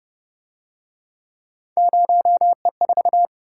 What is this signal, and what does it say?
Morse code, heard as a keyed tone, about 29 wpm, 700 Hz. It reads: 0E4
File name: Speed-Racing - Three Letter-Number Combinations - 25wpm